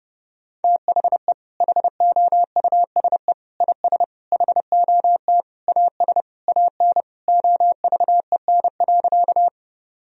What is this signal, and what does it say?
Morse code, heard as a keyed tone, about 32 wpm, 700 Hz. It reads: THE HOUSE IS HOT AS AN OVEN.
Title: Sentences from Top 1000 Words - Review Part 4 - 30wpm